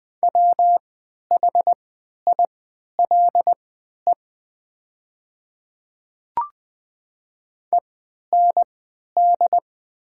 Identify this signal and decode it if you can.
Morse code, heard as a keyed tone, about 20 wpm, spaced as for 11 wpm, 700 Hz. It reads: WHILE END
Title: 3x Character Spacing - Top 300 Words - 20wpm